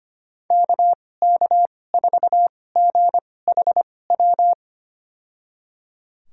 Morse code, heard as a keyed tone, about 25 wpm, spaced as for 18 wpm, 700 Hz. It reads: KK4GHW
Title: USA Call Signs - 2x Character Spacing - 25wpm